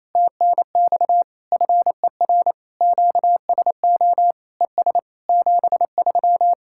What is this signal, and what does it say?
Morse code, heard as a keyed tone, about 27 wpm, 700 Hz. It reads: TNX FER QSO ES 73
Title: Speed-Racing Common QSO Phrases - 28wpm